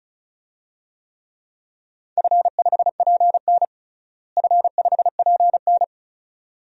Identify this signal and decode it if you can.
Morse code, heard as a keyed tone, about 36 wpm, 700 Hz. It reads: F5PN F5PN
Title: DXCC-100 Call Signs - 4 Characters - Repeated 2x - 35wpm